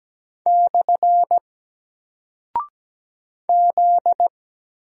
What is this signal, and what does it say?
Morse code, heard as a keyed tone, about 17 wpm, 700 Hz. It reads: / Z
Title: Single Character (Letter:Number:Punctuation:Prosign) 17wpm